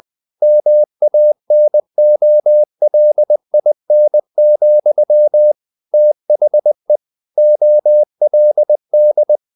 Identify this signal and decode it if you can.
Morse code, heard as a keyed tone, about 20 wpm, 600 Hz. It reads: MANOLIN, THE OLD
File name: the old man and the sea / the old man and the sea part 1 20wpm